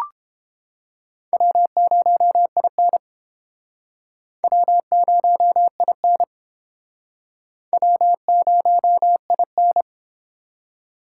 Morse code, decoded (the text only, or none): W0IN W0IN W0IN